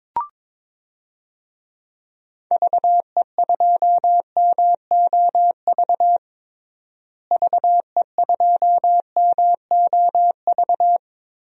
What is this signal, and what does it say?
Morse code, heard as a keyed tone, about 22 wpm, 700 Hz. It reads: VE2MOV VE2MOV
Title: DXCC-100 Call Signs - 6 Characters - Repeated 3x - 22wpm